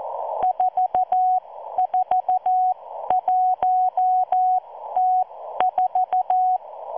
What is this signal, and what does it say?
Morse code, heard as a keyed tone, about 14 wpm, spaced as for 12 wpm, 740 Hz. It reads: 441T4